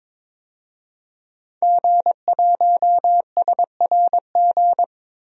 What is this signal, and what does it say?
Morse code, heard as a keyed tone, about 22 wpm, 700 Hz. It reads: G1SRG
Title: DXCC-100 Call Signs - 5 Characters - Repeated 2x - 22wpm